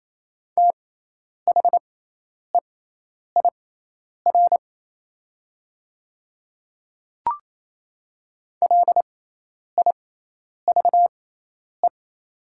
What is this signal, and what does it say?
Morse code, heard as a keyed tone, about 28 wpm, spaced as for 10 wpm, 700 Hz. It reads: THEIR LIVE
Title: 6x Character Spacing - Top 200 Words - 28wpm